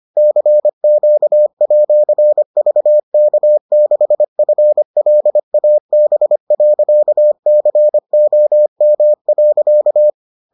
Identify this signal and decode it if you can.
Morse code, heard as a keyed tone, about 25 wpm, 600 Hz. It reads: CQ@VK6FLAB.COM.